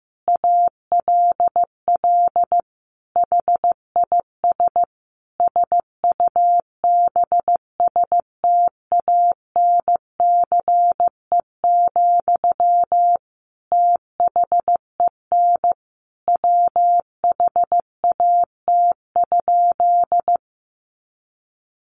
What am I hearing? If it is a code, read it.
Morse code, heard as a keyed tone, about 15 wpm, 700 Hz. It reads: ALL HIS SUBSTANCE, THEN WHAT?
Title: Sentences from Top 1500 Words 15wpm